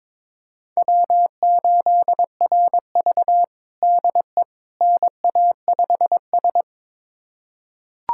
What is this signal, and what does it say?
Morse code, heard as a keyed tone, about 22 wpm, 700 Hz. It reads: W8RV DE NA5S
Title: USA Callsign DE Callsign - 4 Characters - 22wpm